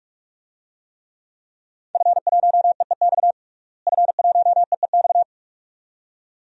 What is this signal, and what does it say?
Morse code, heard as a keyed tone, about 46 wpm, 700 Hz. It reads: F1EEX F1EEX
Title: DXCC-100 Call Signs - 5 Characters - Repeated 2x - 45wpm